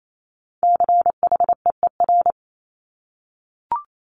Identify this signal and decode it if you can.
Morse code, heard as a keyed tone, about 28 wpm, 700 Hz. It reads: CHEER
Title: Multi-letter Phonemes Repeated 3x 28wpm